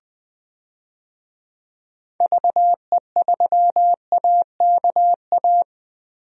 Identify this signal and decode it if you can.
Morse code, heard as a keyed tone, about 21 wpm, 700 Hz. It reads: VE3AKA
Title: DXCC-100 Call Signs - 6 Characters - Repeated 3x - 20wpm